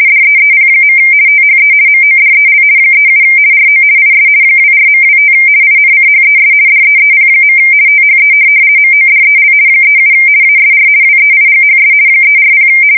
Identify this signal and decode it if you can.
RTTY, about 50 baud. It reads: DEEP. THE MODEL 37 RO IS 24.25 INCHES DEEP. THE MODEL 37 ASR WEIGHS 340
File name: fsk170-50B / teletype-corp